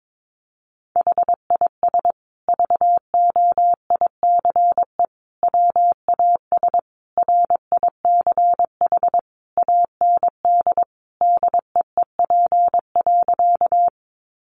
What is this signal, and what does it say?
Morse code, heard as a keyed tone, about 22 wpm, 700 Hz. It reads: HIS VOICE WAS RICH AND DEEP.
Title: Sentences from Top 1100 Words 22wpm